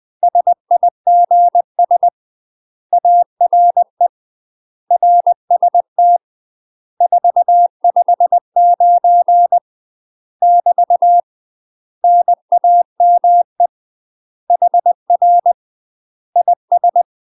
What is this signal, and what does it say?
Morse code, heard as a keyed tone, about 20 wpm, 700 Hz. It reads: SIGS ARE RST 459 <BT> NAME HR IS
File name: UK MASTER QSOs TRAINING / 20WPM / UK MASTER QSOs 0066